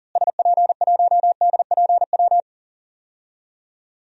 Morse code, heard as a keyed tone, about 40 wpm, 700 Hz. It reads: SP1DPW
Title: DXCC-100 Call Signs - 6 Characters - Repeated 2x - 40wpm